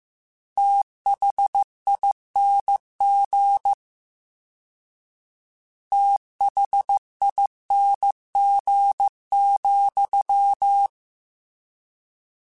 Morse code, heard as a keyed tone, about 15 wpm, 800 Hz. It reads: THING THING,
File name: Combinations / Combinations-mp3s / Combinations@13